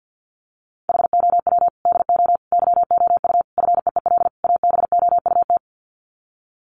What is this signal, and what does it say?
Morse code, heard as a keyed tone, about 50 wpm, 700 Hz. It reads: HOW DO YOU FEEL ABOUT
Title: Sets of 5 Words 50wpm